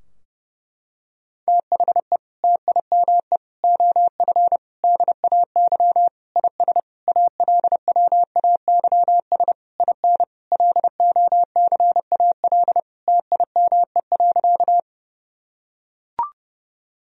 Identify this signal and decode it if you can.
Morse code, heard as a keyed tone, about 30 wpm, 700 Hz. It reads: THE TIME OF DAY IS ALWAYS IN LOCAL TIME.
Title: Sentences from Top 1000 Words - Review Part 3 - 30wpm